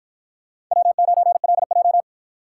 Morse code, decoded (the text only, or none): W9LJ